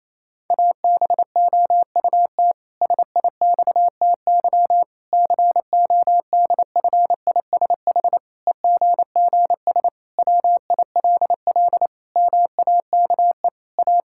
Morse code, heard as a keyed tone, about 27 wpm, 700 Hz. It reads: ABOUT SIXTY CODFISH EGGS WILL MAKE A
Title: Pangram Challenge v2 - 28wpm